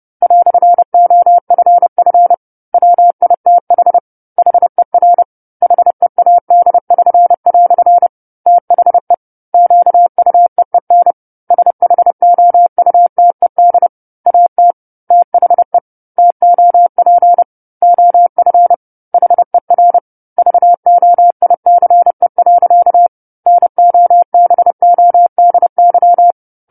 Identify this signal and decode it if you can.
Morse code, heard as a keyed tone, about 30 wpm, 700 Hz. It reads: "OFF WITH HER HEAD<SN>" THE QUEEN SHOUTED AT THE TOP OF HER VOICE. NOBODY
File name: 10-Chapter-10 30 WPM